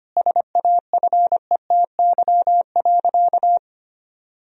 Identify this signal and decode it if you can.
Morse code, heard as a keyed tone, about 25 wpm, 700 Hz. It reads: SAFETY.
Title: Sentences from Top 1000 Words - Review Part 4 - 25wpm